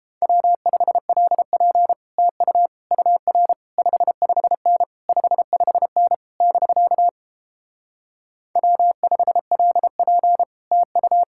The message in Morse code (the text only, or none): W5LP TU UR 55N 55N <BK> W5LP TU